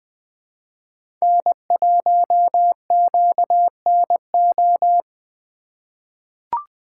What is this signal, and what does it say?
Morse code, heard as a keyed tone, about 20 wpm, 700 Hz. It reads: N1QNO
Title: USA Call Signs - 5 Characters - 3x Character Spacing - 20wpm